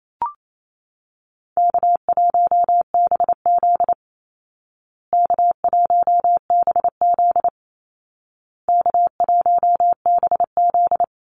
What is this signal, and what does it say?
Morse code, heard as a keyed tone, about 28 wpm, 700 Hz. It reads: K1BZ K1BZ K1BZ
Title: Easy Call Signs Repeated 3x - 28wpm